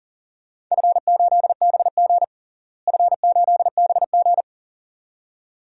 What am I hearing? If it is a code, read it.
Morse code, heard as a keyed tone, about 40 wpm, 700 Hz. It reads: F8BG F8BG
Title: DXCC-100 Call Signs - 4 Characters - Repeated 2x - 40wpm